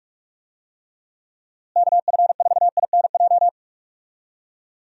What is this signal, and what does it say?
Morse code, heard as a keyed tone, about 41 wpm, 700 Hz. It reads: KF4INJ
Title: USA Call Signs - 6 Characters - Repeated 3x - 45wpm